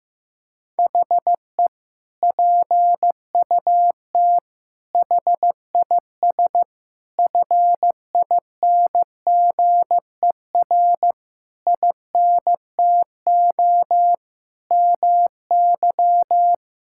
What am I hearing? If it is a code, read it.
Morse code, heard as a keyed tone, about 15 wpm, 700 Hz. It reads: HE PUT HIS FINGER INTO MY
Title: Sentences from Top 1600 Words 15wpm